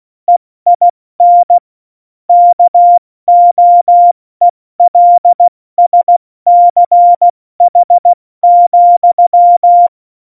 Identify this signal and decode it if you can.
Morse code, heard as a keyed tone, about 16 wpm, 700 Hz. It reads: EIN KOELSCH,
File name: Morsekurs / Witze / Witz 10.Dezember 2019, 70 F80